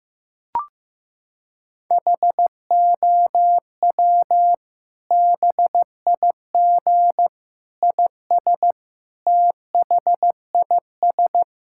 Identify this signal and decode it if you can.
Morse code, heard as a keyed tone, about 15 wpm, 700 Hz. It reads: HOW BIG IS THIS
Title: Sentences from Top 1600 Words 15wpm